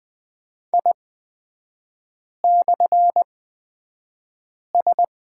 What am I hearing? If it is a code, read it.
Morse code, heard as a keyed tone, about 20 wpm, spaced as for 5 wpm, 700 Hz. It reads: I/S
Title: 236 Lesson 38e - Letter q - Three Characters 20wpm